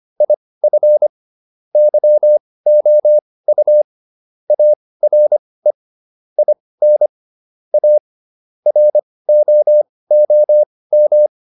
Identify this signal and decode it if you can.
Morse code, heard as a keyed tone, about 25 wpm, spaced as for 18 wpm, 600 Hz. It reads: IF YOU ARE IN A ROOM